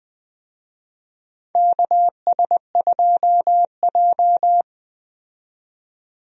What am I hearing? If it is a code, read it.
Morse code, heard as a keyed tone, about 20 wpm, 700 Hz. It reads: KS2J
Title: USA Call Signs - 4 Characters - 20wpm